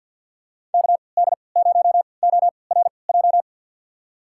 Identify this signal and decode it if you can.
Morse code, heard as a keyed tone, about 46 wpm, spaced as for 27 wpm, 700 Hz. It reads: KD0ORJ
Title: USA Call Signs - 6 Characters - 3x Character Spacing - 50wpm